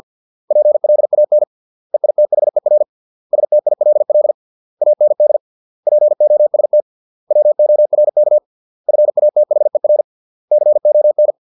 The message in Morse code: PLAN EITHER STILL AND POST WORK FATHER CON

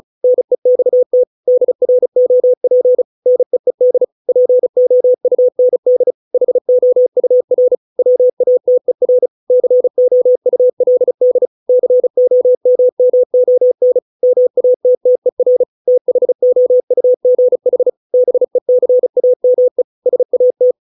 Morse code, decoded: NEXT DROP NEED POUND HOUR WATER COULD COMMON MATTER THOUGH BECAME SAT

